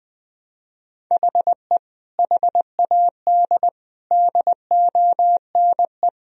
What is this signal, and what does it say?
Morse code, heard as a keyed tone, about 20 wpm, 700 Hz. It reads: HE HAD DONE
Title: Sentences from Top 1000 Words - Review Part 3 - 20wpm